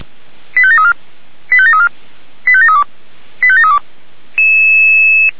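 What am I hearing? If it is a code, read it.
Five-tone selective call (ZVEI), heard as quick runs of five tones, tones of about 70 ms. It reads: 85734 85734 85732 85732
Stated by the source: ZVEI five-tone selective call 85751 received by SDR, logged by Feuerwehr Orlamünde.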